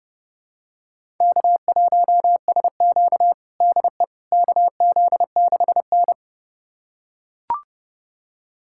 Morse code, heard as a keyed tone, about 30 wpm, 700 Hz. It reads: K1SQ DE KZ6N
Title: USA Callsign DE Callsign - 4 Characters - 30wpm